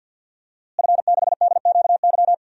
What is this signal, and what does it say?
Morse code, heard as a keyed tone, about 51 wpm, 700 Hz. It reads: F6DQY
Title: DXCC-100 Call Signs - 5 Characters - Repeated 3x - 50wpm